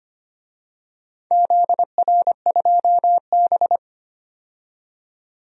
Morse code, heard as a keyed tone, about 25 wpm, 690 Hz. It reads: ZR2B
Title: DXCC-100 Call Signs - 4 Characters - 25wpm